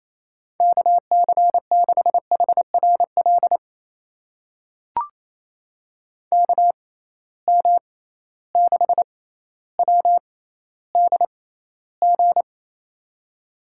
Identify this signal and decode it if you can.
Morse code, heard as a keyed tone, about 28 wpm, spaced as for 14 wpm, 700 Hz. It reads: KC6HRL KM6WDG
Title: USA Call Signs - 6 Characters - 6x Character Spacing - 28wpm